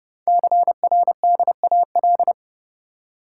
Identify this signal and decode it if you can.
Morse code, heard as a keyed tone, about 29 wpm, 700 Hz. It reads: CRDAL
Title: Code Groups - 5 Letters - 30wpm